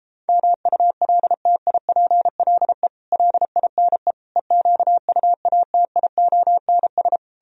Morse code, heard as a keyed tone, about 37 wpm, 700 Hz. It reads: MULTIPLE LINE EQUATIONS